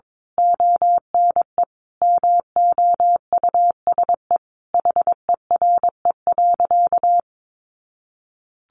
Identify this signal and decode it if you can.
Morse code, heard as a keyed tone, about 22 wpm, 700 Hz. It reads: ONE MOUSE HERE.
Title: A Sound Of Thunder 22wpm